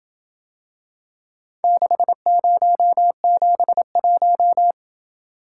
Morse code, heard as a keyed tone, about 27 wpm, 700 Hz. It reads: 6071